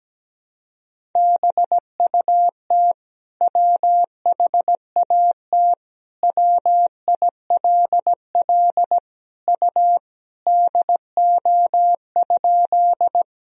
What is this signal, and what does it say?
Morse code, heard as a keyed tone, about 17 wpm, 700 Hz. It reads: BUT WHAT WILL U DO?
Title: Sentences from Top 100 Words and Abbreviations 17wpm